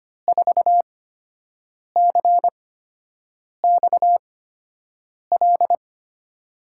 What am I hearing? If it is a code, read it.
Morse code, heard as a keyed tone, about 25 wpm, spaced as for 7 wpm, 700 Hz. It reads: 4CXL
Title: USA Call Signs - 6 Characters - 8x Character Spacing - 25wpm